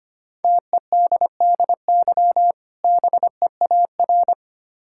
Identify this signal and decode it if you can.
Morse code, heard as a keyed tone, about 25 wpm, 700 Hz. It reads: TEDDY BEAR
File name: practice / 25wpm / Difficult Words with Hints - 25wpm